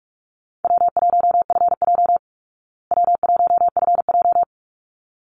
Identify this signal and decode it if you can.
Morse code, heard as a keyed tone, about 44 wpm, 700 Hz. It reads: W1FJ W1FJ
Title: Easy Call Signs Repeated 2x - 45wpm